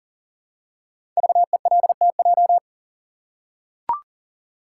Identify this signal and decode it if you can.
Morse code, heard as a keyed tone, about 37 wpm, 700 Hz. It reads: VELTJ
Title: Code Groups - 5 Letter:Numbers - 40wpm